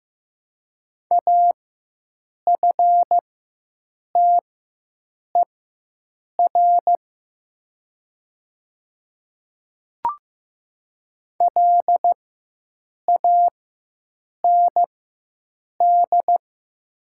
Morse code, heard as a keyed tone, about 15 wpm, spaced as for 7 wpm, 700 Hz. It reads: AFTER LAND
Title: 4x Character Spacing - Top 300 Words - 15wpm